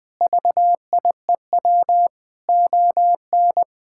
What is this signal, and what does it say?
Morse code, heard as a keyed tone, about 21 wpm, 700 Hz. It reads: VIEW ON
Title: Sentences from Top 1000 Words - Review Part 6 - 20wpm